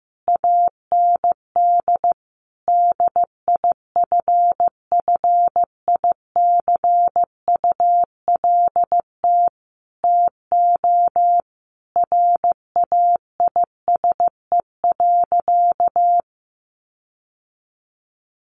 Morse code, heard as a keyed tone, about 15 wpm, 700 Hz. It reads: AND DIFFICULT TO RAISE.
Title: Sentences from Top 1400 Words 15wpm